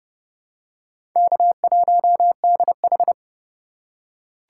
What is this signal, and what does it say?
Morse code, heard as a keyed tone, about 30 wpm, 700 Hz. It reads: K1DH
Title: USA Call Signs - 2x Character Spacing - 30wpm